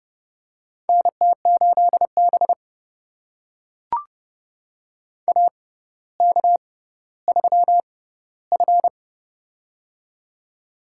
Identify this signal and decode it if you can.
Morse code, heard as a keyed tone, about 30 wpm, spaced as for 15 wpm, 700 Hz. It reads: NT8B AK3F
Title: USA Call Signs - 6x Character Spacing - 30wpm